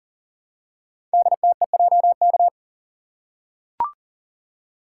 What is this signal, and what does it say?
Morse code, heard as a keyed tone, about 42 wpm, 690 Hz. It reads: DTEJK
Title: Code Groups - 5 Letters - 40wpm